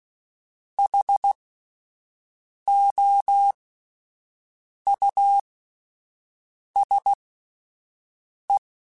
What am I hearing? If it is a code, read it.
Morse code, heard as a keyed tone, about 16 wpm, spaced as for 5 wpm, 800 Hz. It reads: HOUSE
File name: General / General-mp3s / Houses @10